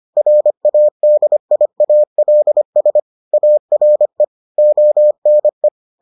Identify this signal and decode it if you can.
Morse code, heard as a keyed tone, about 25 wpm, 600 Hz. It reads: RADIALS ARE ONE